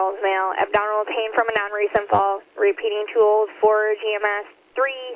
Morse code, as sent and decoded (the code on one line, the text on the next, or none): none